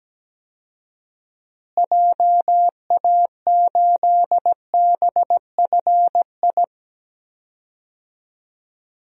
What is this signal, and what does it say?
Morse code, heard as a keyed tone, about 17 wpm, 700 Hz. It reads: JA8BFI
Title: DXCC-100 Call Signs - 6 Characters - Repeated 3x - 17wpm